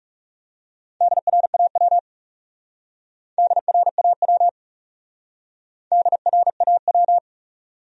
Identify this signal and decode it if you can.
Morse code, heard as a keyed tone, about 37 wpm, 700 Hz. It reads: DRAW DRAW DRAW
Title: Speed-Racing Top 500 Words 30wpm